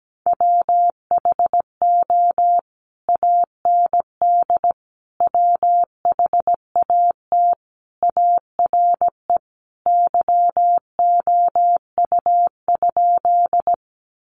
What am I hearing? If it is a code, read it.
Morse code, heard as a keyed tone, about 17 wpm, 700 Hz. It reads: WHO AND WHAT ARE YOU?
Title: Sentences from Top 100 Words 17wpm